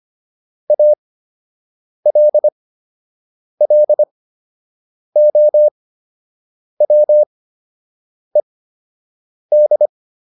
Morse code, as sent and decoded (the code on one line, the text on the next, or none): .- .-.. .-.. --- .-- . -..
ALLOWED